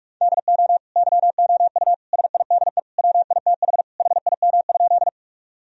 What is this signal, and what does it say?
Morse code, heard as a keyed tone, about 43 wpm, 700 Hz. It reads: DO YOU SIDE WITH HIM?
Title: Sentences from Top 1000 Words - Review Part 1 - 45wpm